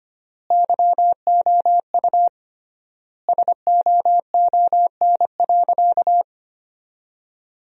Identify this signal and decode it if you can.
Morse code, heard as a keyed tone, about 25 wpm, 700 Hz. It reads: YOU SOON.
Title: 3x Word-Spacing - Sentences from Top 400 Words - 25wpm